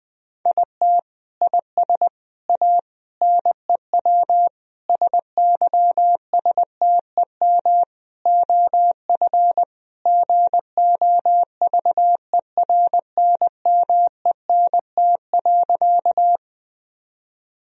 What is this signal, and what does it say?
Morse code, heard as a keyed tone, about 20 wpm, 700 Hz. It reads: IT IS A NEW SYSTEM OF GOVERNMENT.
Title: Sentences from Top 1000 Words - Review Part 3 - 20wpm